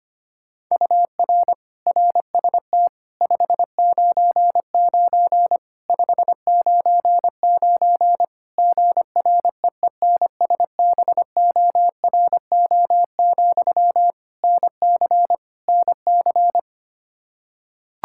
Morse code, heard as a keyed tone, about 25 wpm, 700 Hz. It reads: UR RST 599 599 GREENSBORO, NC NC
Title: K1USN SST QSO Practice 25wpm